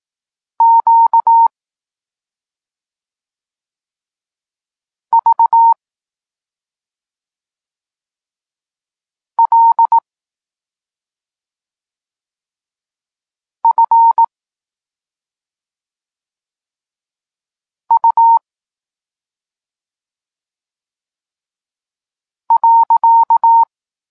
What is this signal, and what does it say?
Morse code, heard as a keyed tone, about 18 wpm, 920 Hz. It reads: Q V L F U .
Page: Tekst 32 – Doskonalenie odbioru poznanych liter